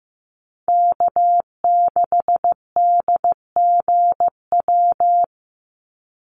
Morse code, heard as a keyed tone, about 15 wpm, 700 Hz. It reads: K6DGW
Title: Call Signs Repeated 3x - 15wpm